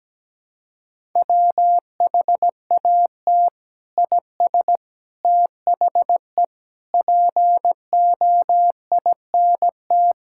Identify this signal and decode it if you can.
Morse code, heard as a keyed tone, about 17 wpm, 700 Hz. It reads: WHAT IS THE POINT
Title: Sentences from Top 1000 Words - Review Part 5 - 17wpm